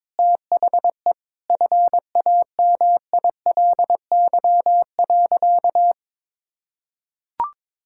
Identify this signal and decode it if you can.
Morse code, heard as a keyed tone, about 22 wpm, 700 Hz. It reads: THE FAMILY.
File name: practice / 22wpm / Sentences with 8-Word Phrases - 22wpm